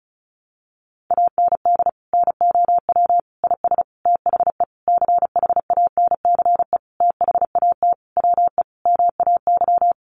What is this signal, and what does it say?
Morse code, heard as a keyed tone, about 34 wpm, 700 Hz. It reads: AND NOW IS THE CHANCE THAT WE MAY
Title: Sentences from Top 800 Words 35wpm